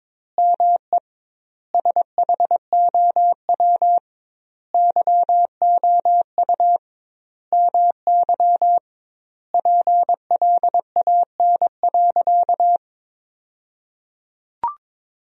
Morse code, heard as a keyed tone, about 22 wpm, 700 Hz. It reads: ME SHOW YOU MY PLAN.
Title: 2x Word-Spacing - Sentences from Top 500 Words - 22wpm